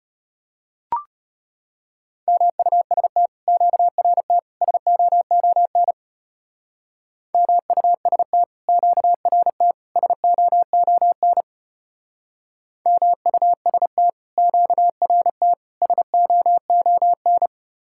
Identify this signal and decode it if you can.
Morse code, heard as a keyed tone, about 33 wpm, 700 Hz. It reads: MUST QRT SOON MUST QRT SOON MUST QRT SOON